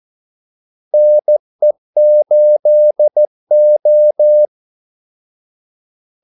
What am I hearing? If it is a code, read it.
Morse code, heard as a keyed tone, about 14 wpm, 600 Hz. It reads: NE8O
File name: Callsigns-14wpm